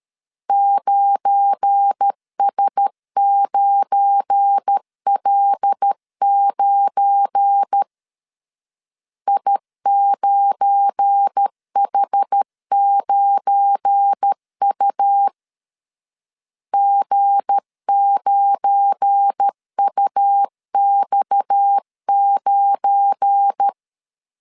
Morse code, heard as a keyed tone, about 13 wpm, 800 Hz. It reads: 9S9L9 I9H9U G9UX9
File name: LEZIONI / 7 GF9Z / 7 GF9 12wpm 2319